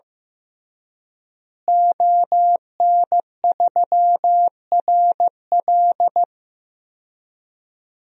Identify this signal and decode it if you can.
Morse code, heard as a keyed tone, about 15 wpm, 700 Hz. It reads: ON3RL
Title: DXCC-100 Call Signs - 5 Characters - Repeated 3x - 15wpm